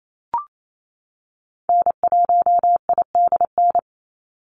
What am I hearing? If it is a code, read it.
Morse code, heard as a keyed tone, about 28 wpm, 700 Hz. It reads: N1IDN
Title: NA SOTA Activator Callsigns 28wpm